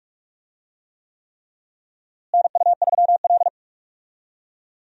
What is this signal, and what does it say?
Morse code, heard as a keyed tone, about 46 wpm, 700 Hz. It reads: NU3L
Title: USA Call Signs - 4 Characters - Repeated 3x - 45wpm